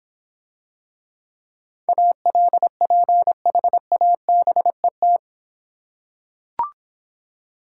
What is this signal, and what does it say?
Morse code, heard as a keyed tone, about 24 wpm, 700 Hz. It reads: ALPHABET
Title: Easy Words - Speed-Racing with Hints - 20wpm